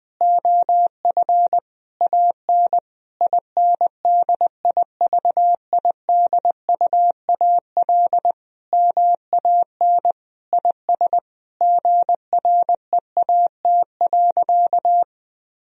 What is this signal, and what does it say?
Morse code, heard as a keyed tone, about 20 wpm, 700 Hz. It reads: OF AN INDIVIDUAL MAN IS GREAT.
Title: Sentences from Top 1000 Words - Review Part 4 - 20wpm